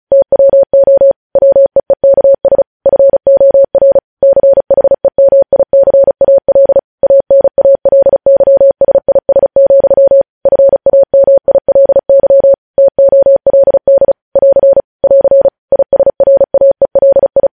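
Morse code, heard as a keyed tone, about 36 wpm, 550 Hz. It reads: TWO WEEKS FOR CHEMICAL ANALYSIS, FAMILY TOLD <AR> <AR> ISRAELI
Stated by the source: Morse code news headlines sent at 35.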